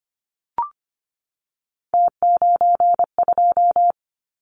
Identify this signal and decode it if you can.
Morse code, heard as a keyed tone, about 25 wpm, 700 Hz. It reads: T92
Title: Callsign Prefixes for Top 50 DXCC - 25wpm